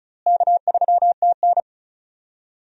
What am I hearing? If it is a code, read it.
Morse code, heard as a keyed tone, about 36 wpm, 700 Hz. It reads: K3TN
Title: USA Call Signs - 4 Characters - Repeated 3x - 35wpm